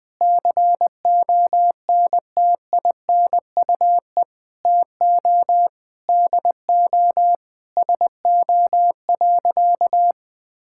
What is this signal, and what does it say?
Morse code, heard as a keyed tone, about 20 wpm, 700 Hz. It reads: CONTINUE TO DO SO.
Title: Sentences with 6-Word Phrases - 20wpm